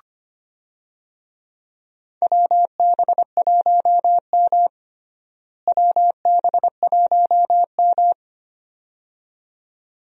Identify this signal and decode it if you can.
Morse code, heard as a keyed tone, about 25 wpm, 700 Hz. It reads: WB1M WB1M
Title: USA Call Signs - 4 Characters - Repeated 2x - 25wpm